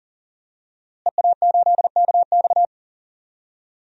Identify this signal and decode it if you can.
Morse code, heard as a keyed tone, about 40 wpm, 700 Hz. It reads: EA8KX
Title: DXCC-100 Call Signs - 5 Characters - Repeated 3x - 40wpm